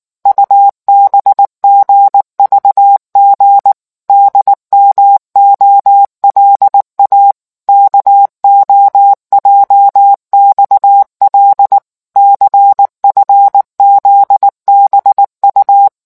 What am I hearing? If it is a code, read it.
Morse code, heard as a keyed tone, about 19 wpm, 800 Hz. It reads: UBGVG DMOLA KOJXL CFZBU